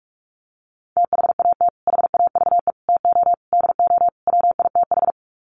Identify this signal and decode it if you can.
Morse code, heard as a keyed tone, about 44 wpm, 700 Hz. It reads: THAT HAVE TO DO WITH
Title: Sets of 5 Words Encore2 45wpm